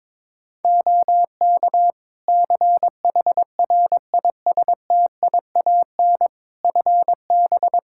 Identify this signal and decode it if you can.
Morse code, heard as a keyed tone, about 22 wpm, 700 Hz. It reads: OK CHRISTIAN FB
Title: Practice QSO - Exchange 2 - Info - 22wpm